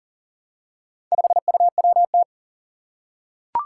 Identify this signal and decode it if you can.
Morse code, heard as a keyed tone, about 40 wpm, 700 Hz. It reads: 5UWT
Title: Four Challenging Letter-Number Combinations 40wpm